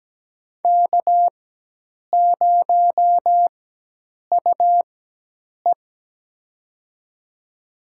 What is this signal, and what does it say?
Morse code, heard as a keyed tone, about 17 wpm, spaced as for 8 wpm, 700 Hz. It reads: K0UE
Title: USA Call Signs - 4 Characters - 4x Character Spacing - 17wpm